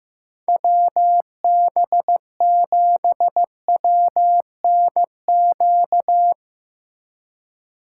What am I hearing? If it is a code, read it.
Morse code, heard as a keyed tone, about 15 wpm, 700 Hz. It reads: WB7WNQ
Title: USA Call Signs - 6 Characters - 3x Character Spacing - 15wpm